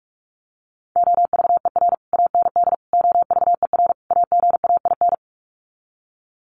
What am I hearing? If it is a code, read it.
Morse code, heard as a keyed tone, about 44 wpm, 700 Hz. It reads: OVER AND OVER AGAIN
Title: Sets of 4 Words 45wpm